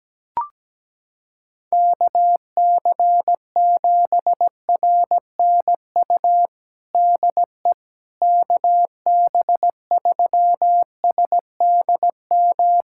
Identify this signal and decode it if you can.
Morse code, heard as a keyed tone, about 17 wpm, 700 Hz. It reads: KC7RNU DE KB3SDM
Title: USA Callsign DE Callsign - 6 Characters - 17wpm